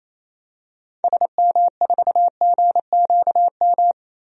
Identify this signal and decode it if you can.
Morse code, heard as a keyed tone, about 28 wpm, 700 Hz. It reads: SM4GQM
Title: DXCC-100 Call Signs - 6 Characters - 28wpm